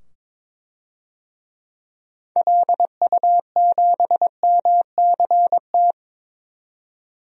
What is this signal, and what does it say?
Morse code, heard as a keyed tone, about 22 wpm, 700 Hz. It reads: LU7MCT
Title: DXCC-100 Call Signs - 6 Characters - 22wpm